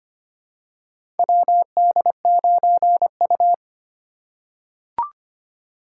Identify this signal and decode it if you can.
Morse code, heard as a keyed tone, about 25 wpm, 700 Hz. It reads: WD9U
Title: USA Call Signs - 4 Characters - 4x Character Spacing - 25wpm